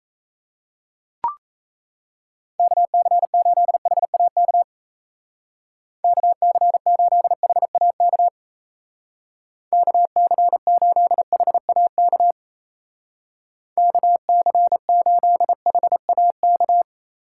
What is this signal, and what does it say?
Morse code, heard as a keyed tone, about 34 wpm, 700 Hz. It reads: KC8HAK KC8HAK KC8HAK KC8HAK
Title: Speed Racing - USA Call Signs 28wpm